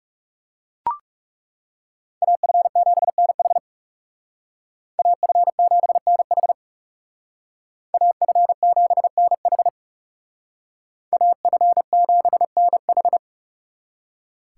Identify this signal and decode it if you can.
Morse code, heard as a keyed tone, about 36 wpm, 700 Hz. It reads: AF7NH AF7NH AF7NH AF7NH
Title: Speed Racing - USA 2x2 Call Signs 30wpm